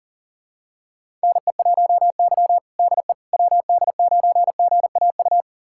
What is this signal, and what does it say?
Morse code, heard as a keyed tone, about 40 wpm, 700 Hz. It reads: NE1Y DE WD9GAU
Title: USA Callsign DE Callsign 40wpm